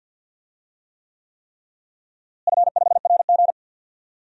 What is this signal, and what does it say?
Morse code, heard as a keyed tone, about 52 wpm, 700 Hz. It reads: F5RG
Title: DXCC-100 Call Signs - 4 Characters - Repeated 3x - 50wpm